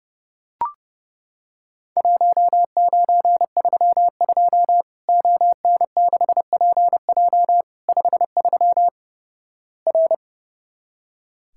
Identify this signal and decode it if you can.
Morse code, heard as a keyed tone, about 29 wpm, 700 Hz. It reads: 1932 ON6PJ 53 R
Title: Worked All Europe DX Contest - QTC Exchange - 30wpm